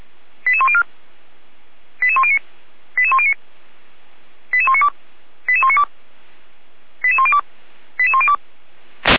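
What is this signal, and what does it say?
Five-tone selective call (ZVEI), heard as quick runs of five tones, tones of about 70 ms. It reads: 88104 88109 88109 88192 88192 88102 88102
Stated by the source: ZVEI five-tone selective call 88103 received by SDR, logged on the Feuerwehr Orlamünde fire brigade server.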